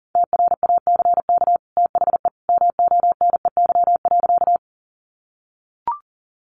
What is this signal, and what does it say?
Morse code, heard as a keyed tone, about 40 wpm, 700 Hz. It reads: TRACK THE MONEY.
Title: Sentences from Top 1100 Words 40wpm